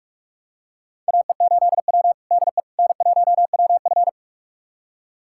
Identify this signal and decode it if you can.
Morse code, heard as a keyed tone, about 42 wpm, 700 Hz. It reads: AE8W DE N1WF
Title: USA Callsign DE Callsign - 4 Characters - 45wpm